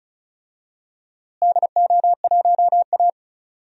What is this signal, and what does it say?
Morse code, heard as a keyed tone, about 36 wpm, 700 Hz. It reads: DO1A
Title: DXCC-100 Call Signs - 4 Characters - Repeated 3x - 35wpm